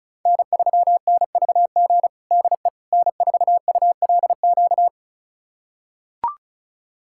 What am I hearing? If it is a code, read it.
Morse code, heard as a keyed tone, about 34 wpm, 700 Hz. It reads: N3NVG DE N4ULQ
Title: USA Callsign DE Callsign - 5 Characters - 35wpm